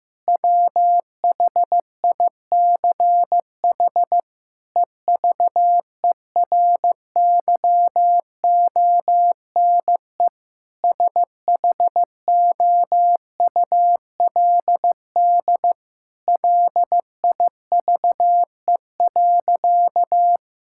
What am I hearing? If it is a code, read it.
Morse code, heard as a keyed tone, about 15 wpm, 700 Hz. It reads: WHICH EVERYONE SHOULD LIVE.
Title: Sentences from Top 1000 Words - Review Part 5 - 15wpm